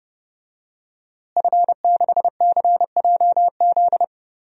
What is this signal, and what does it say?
Morse code, heard as a keyed tone, about 29 wpm, 700 Hz. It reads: F6CJZ
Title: DXCC-100 Call Signs - 5 Characters - Repeated 2x - 30wpm